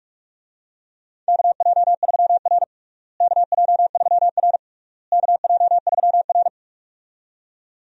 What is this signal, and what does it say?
Morse code, heard as a keyed tone, about 46 wpm, 700 Hz. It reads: KJ3R KJ3R KJ3R
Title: USA Call Signs - 4 Characters - Repeated 3x - 45wpm